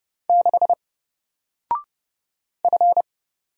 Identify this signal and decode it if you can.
Morse code, heard as a keyed tone, about 31 wpm, 700 Hz. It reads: 6 F
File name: Single-Letter-Number-30wpm